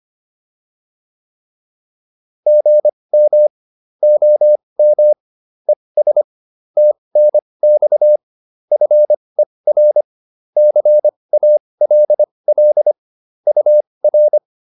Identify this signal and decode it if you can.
Morse code, heard as a keyed tone, about 25 wpm, spaced as for 20 wpm, 600 Hz. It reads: GM OM ES TNX FER CALL UR